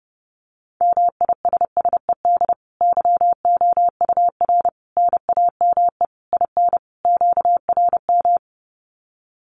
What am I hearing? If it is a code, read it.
Morse code, heard as a keyed tone, about 28 wpm, 700 Hz. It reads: MISSED YOUR NAME IN QRM
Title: Common QSO Phrases 30wpm